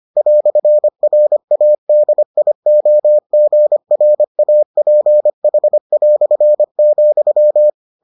Morse code, heard as a keyed tone, about 25 wpm, 600 Hz. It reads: "RADIOGRAPH",